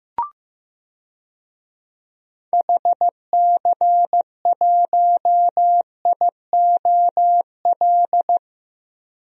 Morse code, heard as a keyed tone, about 15 wpm, 700 Hz. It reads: HC1IOL
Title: DXCC-100 Call Signs - 6 Characters - 15wpm